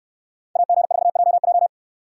Morse code, heard as a keyed tone, about 66 wpm, 690 Hz. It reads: AG4JJ